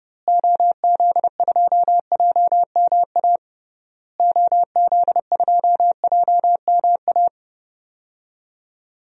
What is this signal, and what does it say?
Morse code, heard as a keyed tone, about 32 wpm, 700 Hz. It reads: OZ2JMA OZ2JMA
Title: DXCC-100 Call Signs - 6 Characters - Repeated 3x - 30wpm